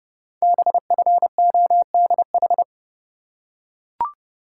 Morse code, heard as a keyed tone, about 30 wpm, 700 Hz. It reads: BFODH
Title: Code Groups - 5 Letters - 30wpm